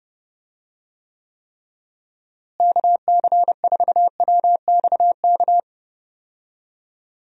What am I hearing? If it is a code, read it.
Morse code, heard as a keyed tone, about 30 wpm, 700 Hz. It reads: KC4WXK
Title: USA Call Signs - 6 Characters - Repeated 3x - 30wpm